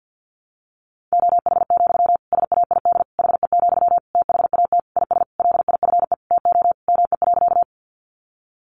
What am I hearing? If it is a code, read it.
Morse code, heard as a keyed tone, about 50 wpm, 700 Hz. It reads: OH, SAID HE, THAT IS LIFE TO ME.
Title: Sentences from Top 500 Words 50wpm